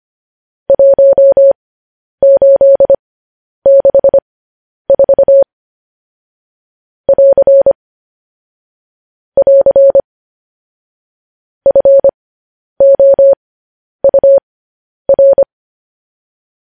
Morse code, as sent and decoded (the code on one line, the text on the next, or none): .---- ---.. -.... ....- / .-.-. / .-.-. / ..-. --- ..- .-.
1864 <AR> <AR> FOUR